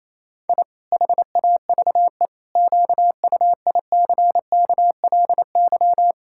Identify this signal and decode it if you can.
Morse code, heard as a keyed tone, about 28 wpm, 700 Hz. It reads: I HAVE QUICKLY
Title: Pangram Challenge v2 - 28wpm